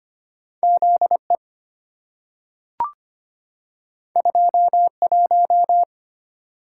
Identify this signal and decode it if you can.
Morse code, heard as a keyed tone, about 25 wpm, 700 Hz. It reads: ZE 21
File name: Two Letter-Number Combinations - 25wpm